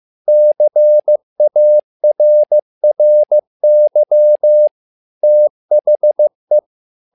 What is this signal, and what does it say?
Morse code, heard as a keyed tone, about 15 wpm, 600 Hz. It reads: CARRY THE
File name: learning center / tower trouble / tower trouble 15wpm